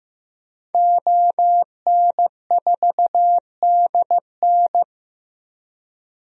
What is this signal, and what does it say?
Morse code, heard as a keyed tone, about 15 wpm, 700 Hz. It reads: ON4DN
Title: DXCC-100 Call Signs - 5 Characters - 15wpm